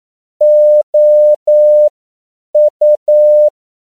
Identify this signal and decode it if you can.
Morse code, heard as a keyed tone, about 9 wpm, spaced as for 7 wpm, 600 Hz. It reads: OU
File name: CW / 10PPM